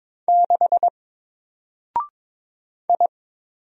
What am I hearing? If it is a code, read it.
Morse code, heard as a keyed tone, about 22 wpm, 700 Hz. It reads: 6 I
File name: Single-Letter-Number-22wpm